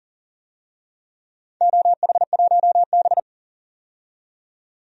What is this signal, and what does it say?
Morse code, heard as a keyed tone, about 40 wpm, 700 Hz. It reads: OH1B
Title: DXCC-100 Call Signs - 4 Characters - 40wpm